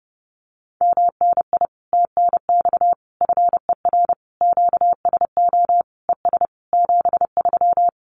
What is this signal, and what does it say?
Morse code, heard as a keyed tone, about 30 wpm, 700 Hz. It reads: MNI TNX FER QSO ES 73
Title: Common QSO Phrases Encore 30wpm